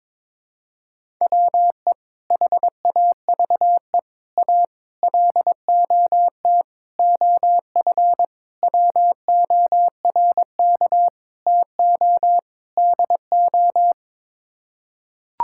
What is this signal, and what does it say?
Morse code, heard as a keyed tone, about 22 wpm, 700 Hz. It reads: WE HAVE A LOT OF WORK TO DO
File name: practice / 22wpm / Sets of 8 Words - 22wpm